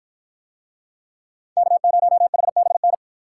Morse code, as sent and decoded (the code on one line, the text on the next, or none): -.- ----- .... -... -.
K0HBN